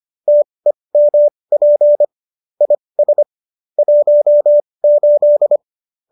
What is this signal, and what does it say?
Morse code, heard as a keyed tone, about 25 wpm, spaced as for 20 wpm, 590 Hz. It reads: TEMP IS 18
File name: qso208 20